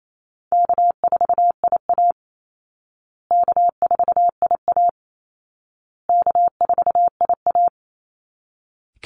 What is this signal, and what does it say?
Morse code, heard as a keyed tone, about 28 wpm, 700 Hz. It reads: K4IA K4IA K4IA
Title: Easy Call Signs Repeated 3x - 28wpm